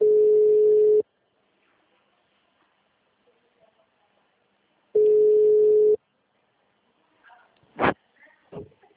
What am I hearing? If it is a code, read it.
Ringback tone (Europe, 425 Hz).
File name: out- -2006- - - .38852